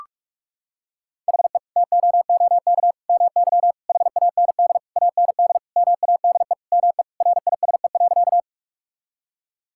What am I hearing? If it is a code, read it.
Morse code, heard as a keyed tone, about 46 wpm, 700 Hz. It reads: HE TOOK MY HAND AND MADE ME RISE.
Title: Sentences from Top 1000 Words - Review Part 6 - 45wpm